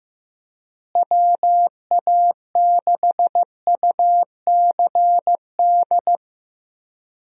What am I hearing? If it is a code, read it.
Morse code, heard as a keyed tone, about 15 wpm, 700 Hz. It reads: WA6UCD
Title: USA Call Signs - 6 Characters - Repeated 3x - 15wpm